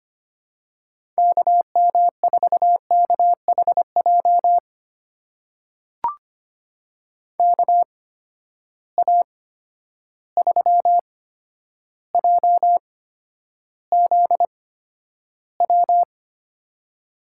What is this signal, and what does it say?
Morse code, heard as a keyed tone, about 25 wpm, spaced as for 11 wpm, 700 Hz. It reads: KM4KHJ KA3JZW
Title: USA Call Signs - 6 Characters - 8x Character Spacing - 25wpm